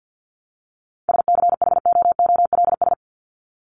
Morse code, heard as a keyed tone, about 50 wpm, 700 Hz. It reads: SCHOOLS
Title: ES:S Suffix - Top 100 Words with Base Words 50wpm